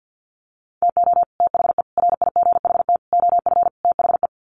Morse code, heard as a keyed tone, about 50 wpm, 700 Hz. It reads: TO THE RIGHT OF THE
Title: Sets of 5 Words Encore 50wpm